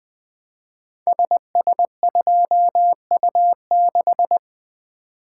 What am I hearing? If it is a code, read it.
Morse code, heard as a keyed tone, about 21 wpm, 700 Hz. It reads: SS2U6
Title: Code Groups - 5 Letter:Numbers - 20wpm